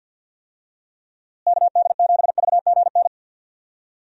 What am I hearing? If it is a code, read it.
Morse code, heard as a keyed tone, about 51 wpm, 700 Hz. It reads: KD7VGN
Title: USA Call Signs - 6 Characters - Repeated 2x - 50wpm